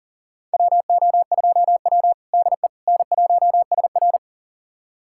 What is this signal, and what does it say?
Morse code, heard as a keyed tone, about 37 wpm, 690 Hz. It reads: WO2W DE N1SR
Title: USA Callsign DE Callsign - 4 Characters - 40wpm